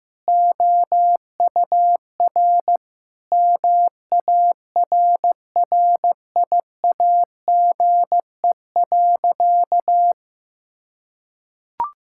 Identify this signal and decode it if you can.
Morse code, heard as a keyed tone, about 15 wpm, 700 Hz. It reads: OUR MARRIAGE.
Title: Sentences from Top 1000 Words - Review Part 5 - 15wpm